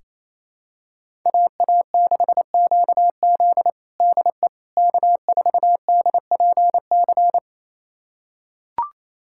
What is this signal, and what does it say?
Morse code, heard as a keyed tone, about 28 wpm, 700 Hz. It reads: AA6QZ DE K4DPC
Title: USA Callsign DE Callsign - 5 Characters - 28wpm